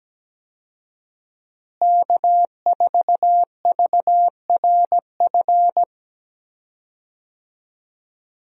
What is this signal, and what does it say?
Morse code, heard as a keyed tone, about 17 wpm, 700 Hz. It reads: K4VRF
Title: USA Call Signs - 5 Characters - Repeated 2x - 17wpm